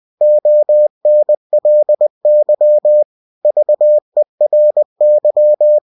Morse code, heard as a keyed tone, about 20 wpm, 600 Hz. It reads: ONLY VERY
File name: cw-books / winnie-the-pooh / winnie-20wpm-page-0036